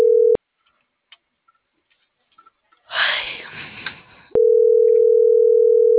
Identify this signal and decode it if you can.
Ringback tone (North America), 440 + 480 Hz.
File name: CENTRAL NMP / out melanie.diaznmp - .39715